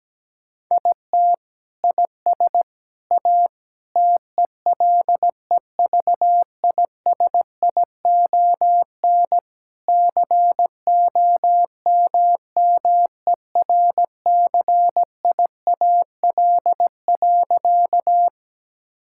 Morse code, heard as a keyed tone, about 17 wpm, 700 Hz. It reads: IT IS A TELEVISION COMMERCIAL.
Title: Sentences from Top 1000 Words - Review Part 5 - 17wpm